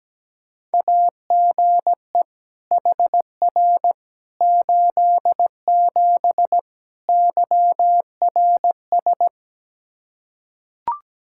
Morse code, heard as a keyed tone, about 17 wpm, 700 Hz. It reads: AGE HR 87 YRS
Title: Practice QSO - Exchange 3 - More Info - 17wpm